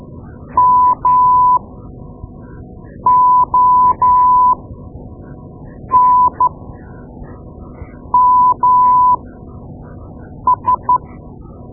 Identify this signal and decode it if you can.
Morse code, heard as a keyed tone, about 13 wpm, spaced as for 5 wpm, 1000 Hz. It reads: MONMS
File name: Morse Lessons / Morse Lessons / Tape-2B